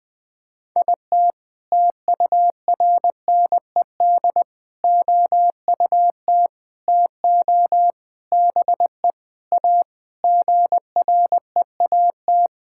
Morse code, heard as a keyed tone, about 20 wpm, 700 Hz. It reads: IT TURNED OUT TO BE A GREAT
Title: Sentences with 5-Word Phrases - Spoken Beginning - 20wpm